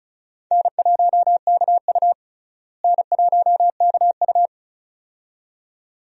Morse code, heard as a keyed tone, about 34 wpm, 700 Hz. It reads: N1KU N1KU
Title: USA Call Signs - 4 Characters - Repeated 2x - 35wpm